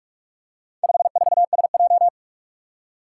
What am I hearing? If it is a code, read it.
Morse code, heard as a keyed tone, about 44 wpm, 700 Hz. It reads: 54SJ
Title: Four Challenging Letter-Number Combinations 45wpm